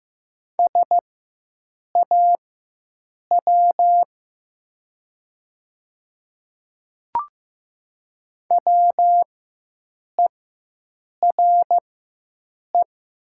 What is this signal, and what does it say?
Morse code, heard as a keyed tone, about 15 wpm, spaced as for 7 wpm, 700 Hz. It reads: SAW WERE
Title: 4x Character Spacing - Top 300 Words - 15wpm